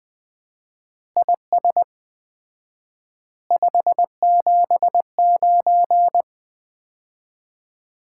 Morse code, heard as a keyed tone, about 20 wpm, 700 Hz. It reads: IS 579
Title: Common QSO Phrases - 4x Word-Spacing - 20wpm